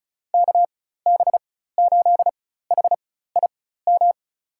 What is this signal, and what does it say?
Morse code, heard as a keyed tone, about 34 wpm, spaced as for 16 wpm, 700 Hz. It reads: KB8HIM
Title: USA Call Signs - 4x Character Spacing - 35wpm